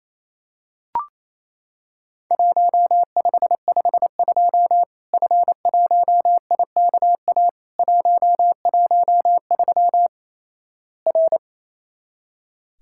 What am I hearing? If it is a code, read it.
Morse code, heard as a keyed tone, about 27 wpm, 700 Hz. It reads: 1552 F1IKA 113 R
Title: Worked All Europe DX Contest - QTC Exchange - 28wpm